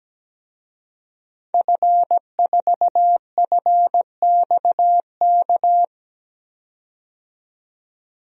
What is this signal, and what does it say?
Morse code, heard as a keyed tone, about 17 wpm, 700 Hz. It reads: F4FXK
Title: DXCC-100 Call Signs - 5 Characters - 17wpm